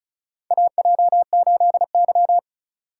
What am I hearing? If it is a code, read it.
Morse code, heard as a keyed tone, about 36 wpm, 700 Hz. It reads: AJ8Y
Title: USA Call Signs - 4 Characters - 4x Character Spacing - 35wpm